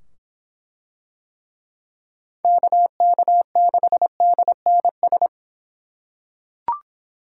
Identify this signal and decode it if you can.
Morse code, heard as a keyed tone, about 25 wpm, 700 Hz. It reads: KK6DNS